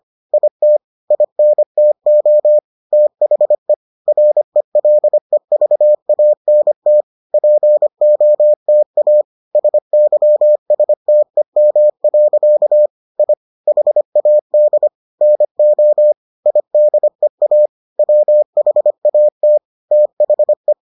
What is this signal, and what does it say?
Morse code, heard as a keyed tone, about 25 wpm, 600 Hz. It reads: IT INTO THE RELEVANT POTA SYSTEM. I HAD NO IDEA WHAT THE